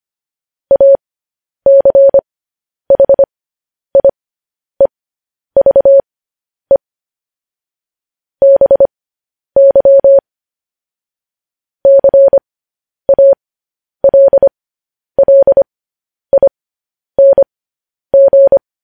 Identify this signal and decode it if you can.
Morse code, heard as a keyed tone, about 25 wpm, spaced as for 10 wpm, 550 Hz. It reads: ACHIEVE BY CALLING
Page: news headlines in morse code at 10 wpm